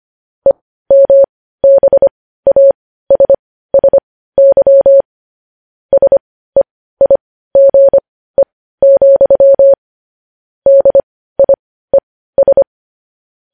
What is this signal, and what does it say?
Morse code, heard as a keyed tone, about 25 wpm, spaced as for 15 wpm, 550 Hz. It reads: EMBASSY SEIGE, DIES
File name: podcast / News headlines-1x-25x15